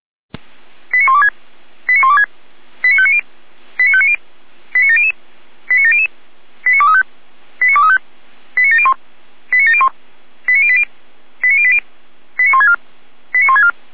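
Five-tone selective call (ZVEI), heard as quick runs of five tones, tones of about 70 ms. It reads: 89126 89126 89590 89590 89700 89700 89235 89235 89701 89701 89080 89080 89164 89164
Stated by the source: ZVEI five-tone selective call 89115 received by SDR, archived by Feuerwehr Orlamünde.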